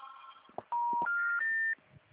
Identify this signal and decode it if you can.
Special information tone.